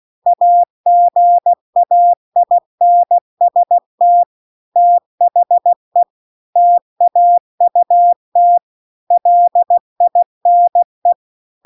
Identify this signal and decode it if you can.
Morse code, heard as a keyed tone, about 16 wpm, 700 Hz. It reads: AGAINST THE TAUT LINE
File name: CW-The-Big-Twoheaded-River-2-16WPM